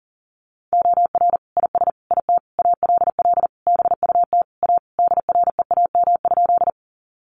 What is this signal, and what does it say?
Morse code, heard as a keyed tone, about 37 wpm, 700 Hz. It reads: OR IS IT ALL BUT A DREAM?
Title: Sentences from Top 1500 Words 40wpm